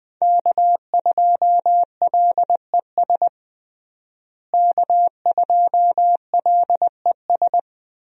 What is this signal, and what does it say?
Morse code, heard as a keyed tone, about 20 wpm, 700 Hz. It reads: K2LES K2LES
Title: USA Call Signs - 5 Characters - Repeated 2x - 20wpm